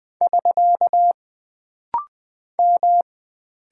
Morse code, heard as a keyed tone, about 20 wpm, 700 Hz. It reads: <SK> M
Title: Single Character (Letter:Number:Punctuation:Prosign) - Rapid-Fire - 20wpm